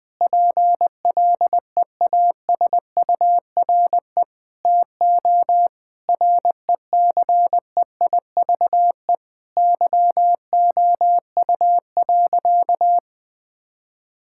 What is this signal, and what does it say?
Morse code, heard as a keyed tone, about 21 wpm, 700 Hz. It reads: PLEASURE TO RECEIVE YOU.
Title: Sentences from Top 1600 Words 20wpm